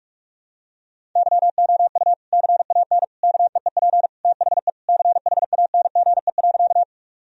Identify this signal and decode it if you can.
Morse code, heard as a keyed tone, about 46 wpm, 700 Hz. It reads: YOU CAN KEEP THE CHANGE.
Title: Sentences from Top 1000 Words - Review Part 1 - 45wpm